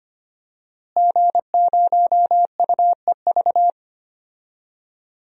Morse code, heard as a keyed tone, about 25 wpm, 700 Hz. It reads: G0UEV
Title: DXCC-100 Call Signs - 5 Characters - Repeated 3x - 25wpm